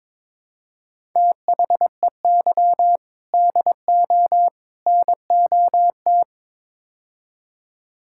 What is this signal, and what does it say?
Morse code, heard as a keyed tone, about 22 wpm, 700 Hz. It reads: THEY DO NOT
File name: practice / 22wpm / Build A Phrase - 5-Word Phrases - 22wpm